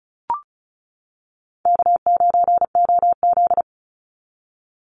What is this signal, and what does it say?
Morse code, heard as a keyed tone, about 36 wpm, 700 Hz. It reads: K9OZ
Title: NA SOTA Chaser Callsigns 35wpm